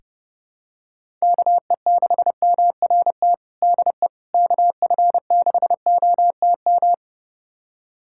Morse code, heard as a keyed tone, about 30 wpm, 700 Hz. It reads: KE6MRT DE KF6OTM
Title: USA Callsign DE Callsign - 6 Characters - 30wpm